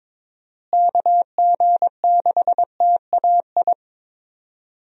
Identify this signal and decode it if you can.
Morse code, heard as a keyed tone, about 22 wpm, 700 Hz. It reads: KG6TAI
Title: USA Call Signs - 6 Characters - Repeated 2x - 22wpm